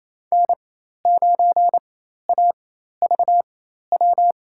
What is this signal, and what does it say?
Morse code, heard as a keyed tone, about 28 wpm, spaced as for 13 wpm, 700 Hz. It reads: N9AVW
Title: USA Call Signs - 5 Characters - 4x Character Spacing - 28wpm